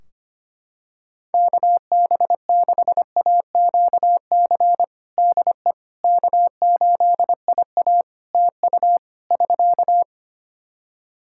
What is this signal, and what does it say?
Morse code, heard as a keyed tone, about 25 wpm, 700 Hz. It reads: KB6AQC DE K8IA TU <SK>
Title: K1USN SST QSO Practice 25wpm